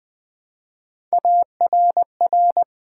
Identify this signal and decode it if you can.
Morse code, heard as a keyed tone, about 20 wpm, 700 Hz. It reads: ARR
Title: Build A Word - 1 Letter At A Time - 5-Letter Words - 20wpm